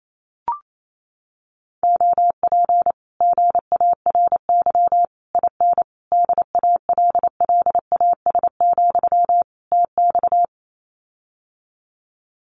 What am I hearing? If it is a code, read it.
Morse code, heard as a keyed tone, about 28 wpm, 700 Hz. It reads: OP GARY IN DALLAS, TX
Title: Common QSO Phrases Encore 28wpm